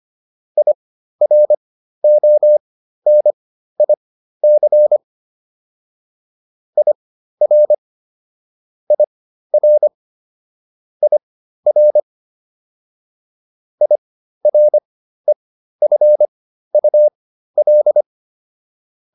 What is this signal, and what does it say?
Morse code, heard as a keyed tone, about 25 wpm, spaced as for 13 wpm, 600 Hz. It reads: IRONIC IR IR IR IREFUL